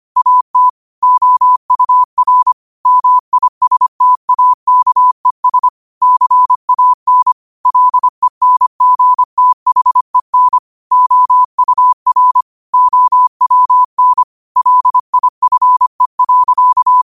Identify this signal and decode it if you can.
Morse code, heard as a keyed tone, about 25 wpm, 1000 Hz. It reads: AT OUR MISTAKES CAN LENGTHEN OUR OWN LIFE.
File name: qotdfiles / 25wpm